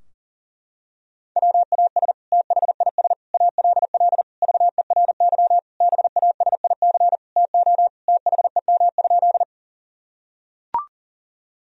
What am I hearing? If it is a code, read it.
Morse code, heard as a keyed tone, about 40 wpm, 700 Hz. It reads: WAS THIS ALL VERY BASIC TO THEM?
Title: Sentences from Top 1000 Words - Review Part 3 - 40wpm